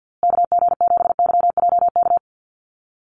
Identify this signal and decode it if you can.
Morse code, heard as a keyed tone, about 48 wpm, 700 Hz. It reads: KG7YJK
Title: USA Call Signs Repeated 3x 50wpm